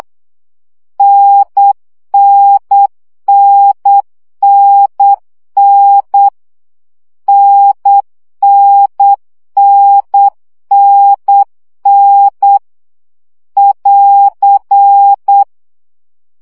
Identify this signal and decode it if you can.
Morse code, heard as a keyed tone, about 8 wpm, 800 Hz. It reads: NNNNN NNNNN <AR>
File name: CW gyakorlas n 7 baud